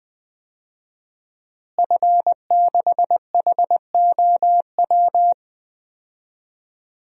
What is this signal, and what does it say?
Morse code, heard as a keyed tone, about 20 wpm, 700 Hz. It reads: F6HOW
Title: DXCC-100 Call Signs - 5 Characters - Repeated 2x - 20wpm